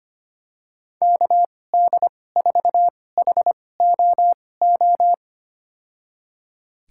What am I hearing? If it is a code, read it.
Morse code, heard as a keyed tone, about 25 wpm, spaced as for 18 wpm, 700 Hz. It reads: KD4HOO
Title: USA Call Signs - 6 Characters - 2x Character Spacing - 25wpm